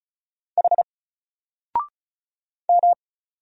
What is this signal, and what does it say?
Morse code, heard as a keyed tone, about 34 wpm, 700 Hz. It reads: H M
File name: Single-Letter-Number-35wpm